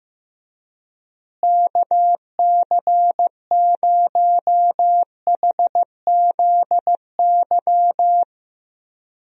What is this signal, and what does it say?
Morse code, heard as a keyed tone, about 15 wpm, 700 Hz. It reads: KC0HZY
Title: USA Call Signs - 6 Characters - Repeated 2x - 15wpm